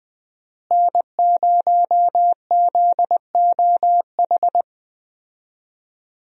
Morse code, heard as a keyed tone, about 20 wpm, 700 Hz. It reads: N0ZOH
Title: USA Call Signs - 5 Characters - Repeated 2x - 20wpm